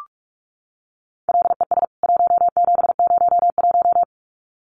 Morse code, heard as a keyed tone, about 44 wpm, 700 Hz. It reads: LES 1701
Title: CWT Exchange 45wpm